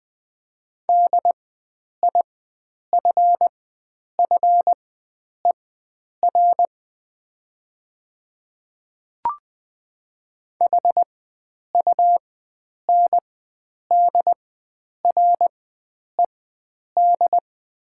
Morse code, heard as a keyed tone, about 20 wpm, spaced as for 9 wpm, 700 Hz. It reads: DIFFER HUNDRED
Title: 4x Character Spacing - Top 500 Words - 20wpm